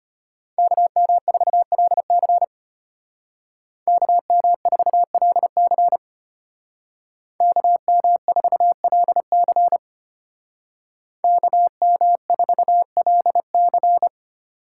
Morse code, heard as a keyed tone, about 33 wpm, 700 Hz. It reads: KM4LC KM4LC KM4LC KM4LC